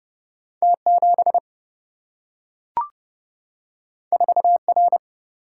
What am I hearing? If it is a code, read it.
Morse code, heard as a keyed tone, about 30 wpm, 700 Hz. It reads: T7 4R
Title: Single-Letter Single-Number 30wpm